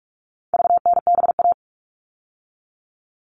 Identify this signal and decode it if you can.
Morse code, heard as a keyed tone, about 44 wpm, 700 Hz. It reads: VNBA
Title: All Cut Numbers - 4-Digit 45wpm